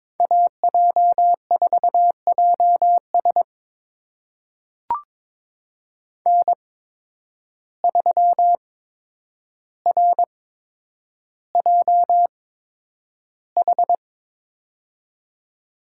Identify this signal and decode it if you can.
Morse code, heard as a keyed tone, about 21 wpm, spaced as for 9 wpm, 700 Hz. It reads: AJ4JS N3RJH
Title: USA Call Signs - 5 Characters - 8x Character Spacing - 22wpm